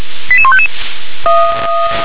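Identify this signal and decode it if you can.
Five-tone selective call (ZVEI), heard as quick runs of five tones, tones of about 70 ms. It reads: 88155
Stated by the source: ZVEI five-tone call 88101, received on SDR.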